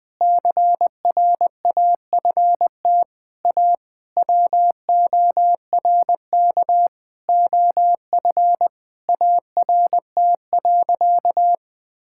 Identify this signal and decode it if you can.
Morse code, heard as a keyed tone, about 20 wpm, 700 Hz. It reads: CRAFT A WORK OF ART.
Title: Sentences from Top 1000 Words - Review Part 6 - 20wpm